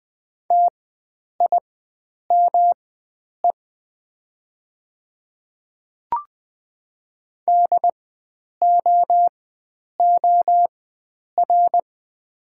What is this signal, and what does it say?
Morse code, heard as a keyed tone, about 20 wpm, spaced as for 9 wpm, 700 Hz. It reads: TIME DOOR
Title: 4x Character Spacing - Top 300 Words - 20wpm